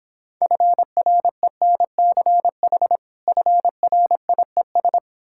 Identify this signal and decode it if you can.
Morse code, heard as a keyed tone, about 26 wpm, 700 Hz. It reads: FRENCH FRIES